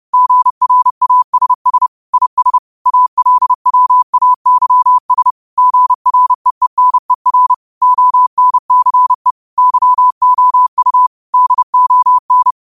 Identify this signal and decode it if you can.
Morse code, heard as a keyed tone, about 30 wpm, 1000 Hz. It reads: GRASS IS ALWAYS GREENER ONCE YOU DON
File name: 30wpm